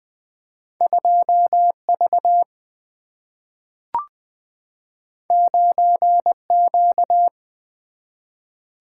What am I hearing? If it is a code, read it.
Morse code, heard as a keyed tone, about 20 wpm, 700 Hz. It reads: 2 V 9 Q
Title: Single-Letter Single-Number 20wpm